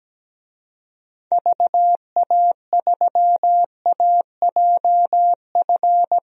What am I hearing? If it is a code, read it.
Morse code, heard as a keyed tone, about 17 wpm, 700 Hz. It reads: VA3AJF
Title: DXCC-100 Call Signs - 6 Characters - Repeated 3x - 17wpm